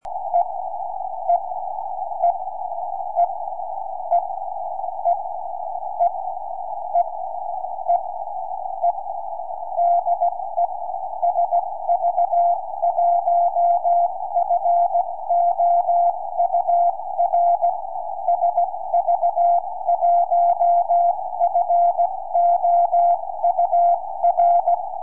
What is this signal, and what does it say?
Morse code, heard as a keyed tone, about 16 wpm, 690 Hz. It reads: E E E E E E E E E E DE SV1FOUR SV1FOUR
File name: sv1four 4m